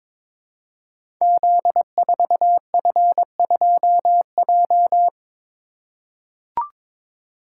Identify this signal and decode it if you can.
Morse code, heard as a keyed tone, about 22 wpm, 700 Hz. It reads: Z4F2J
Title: Code Groups - 5 Letter:Numbers - 22wpm